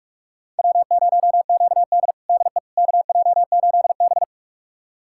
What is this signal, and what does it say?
Morse code, heard as a keyed tone, about 48 wpm, 700 Hz. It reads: W0QD DE KJ8B
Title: USA Callsign DE Callsign - 4 Characters - 45wpm